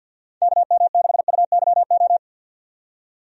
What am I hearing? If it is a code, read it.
Morse code, heard as a keyed tone, about 51 wpm, 700 Hz. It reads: KM6UYO
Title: USA Call Signs - 6 Characters - 50wpm